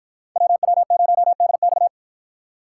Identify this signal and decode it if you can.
Morse code, heard as a keyed tone, about 53 wpm, 700 Hz. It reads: WW0DX